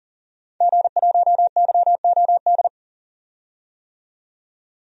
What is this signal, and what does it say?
Morse code, heard as a keyed tone, about 40 wpm, 700 Hz. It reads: G1YOD
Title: DXCC-100 Call Signs - 5 Characters - 40wpm